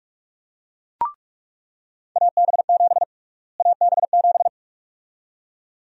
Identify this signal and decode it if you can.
Morse code, heard as a keyed tone, about 46 wpm, 700 Hz. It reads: AB7 AB7
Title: USA Call Sign Prefixes - Repeated 2x - 45wpm